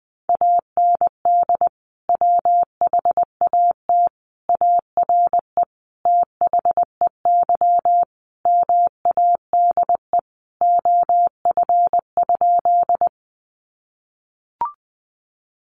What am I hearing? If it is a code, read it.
Morse code, heard as a keyed tone, about 20 wpm, 700 Hz. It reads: AND WHAT ARE THEY MADE OF?
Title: Sentences from Top 200 Words 20wpm